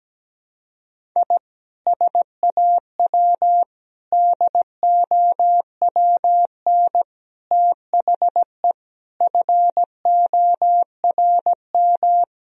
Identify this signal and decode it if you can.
Morse code, heard as a keyed tone, about 17 wpm, 700 Hz. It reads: I SAW DOWN THE FORM
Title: Sentences from Top 1000 Words - Review Part 1 - 17wpm